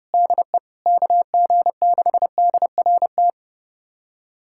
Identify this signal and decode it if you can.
Morse code, heard as a keyed tone, about 30 wpm, 700 Hz. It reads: DE KG6DRT
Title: USA Callsign DE Callsign - 6 Characters - 30wpm